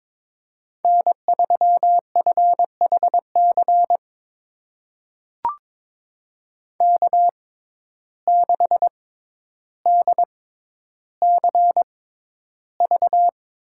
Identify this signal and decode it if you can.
Morse code, heard as a keyed tone, about 22 wpm, spaced as for 11 wpm, 700 Hz. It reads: N3FHC K6DCV
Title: USA Call Signs - 5 Characters - 6x Character Spacing - 22wpm